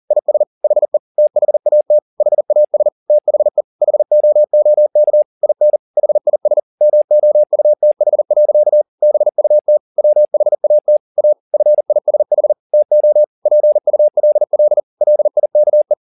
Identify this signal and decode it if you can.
Morse code, heard as a keyed tone, about 40 wpm, 600 Hz. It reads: IS HE THAT HAS THE HOOK IN HIS MOUTH. BUT WHAT A FISH TO PULL LIKE